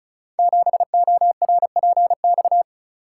Morse code, heard as a keyed tone, about 36 wpm, 700 Hz. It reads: 7ORPX
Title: Code Groups - 5 Letter:Numbers - 35wpm